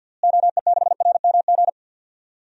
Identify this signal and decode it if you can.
Morse code, heard as a keyed tone, about 50 wpm, 700 Hz. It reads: OE6RMG
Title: DXCC-100 Call Signs - 6 Characters - Repeated 2x - 50wpm